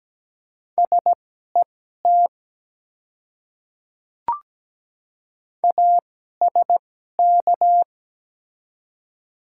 Morse code, heard as a keyed tone, about 17 wpm, spaced as for 12 wpm, 700 Hz. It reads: SET ASK
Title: 2x Character Spacing - Top 200 Words - 17wpm